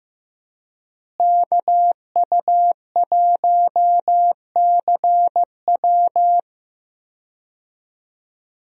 Morse code, heard as a keyed tone, about 15 wpm, 700 Hz. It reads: KU1CW
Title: USA Call Signs - 5 Characters - Repeated 3x - 15wpm